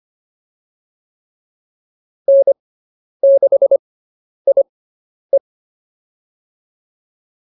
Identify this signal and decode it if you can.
Morse code, heard as a keyed tone, about 25 wpm, spaced as for 10 wpm, 550 Hz. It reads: N6IE
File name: POTA205 10